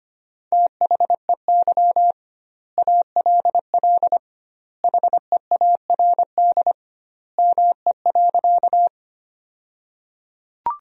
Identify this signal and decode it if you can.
Morse code, heard as a keyed tone, about 25 wpm, 700 Hz. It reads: THEY ALL HEARD ME.
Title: 2x Word-Spacing - Sentences from Top 400 Words - 25wpm